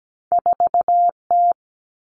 Morse code, heard as a keyed tone, about 17 wpm, 700 Hz. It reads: 4T
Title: Common Cut Numbers - 2-Digit 17wpm